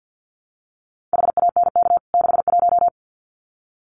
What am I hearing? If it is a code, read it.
Morse code, heard as a keyed tone, about 50 wpm, 700 Hz. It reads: HANK 61
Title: CWT Exchange 50wpm